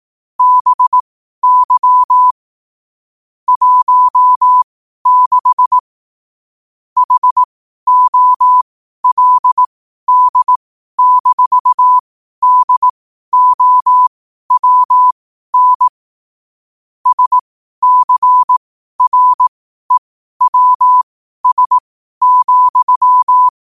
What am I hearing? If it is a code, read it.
Morse code, heard as a keyed tone, about 18 wpm, spaced as for 12 wpm, 1000 Hz. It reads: BY 16 HOLD-DOWN SCREWS,
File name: qotdfiles / 12wpm